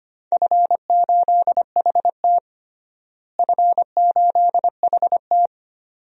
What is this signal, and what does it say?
Morse code, heard as a keyed tone, about 25 wpm, 700 Hz. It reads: F8HT F8HT
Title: DXCC-100 Call Signs - 4 Characters - Repeated 2x - 25wpm